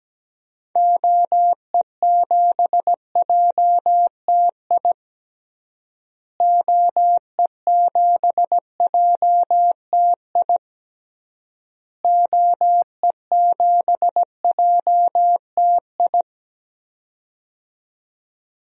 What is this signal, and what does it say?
Morse code, heard as a keyed tone, about 17 wpm, 700 Hz. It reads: OE7JTI OE7JTI OE7JTI
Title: DXCC-100 Call Signs - 6 Characters - Repeated 3x - 17wpm